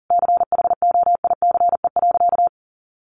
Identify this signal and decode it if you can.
Morse code, heard as a keyed tone, about 40 wpm, 700 Hz. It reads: CHOICE.
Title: Sentences from Top 900 Words 40wpm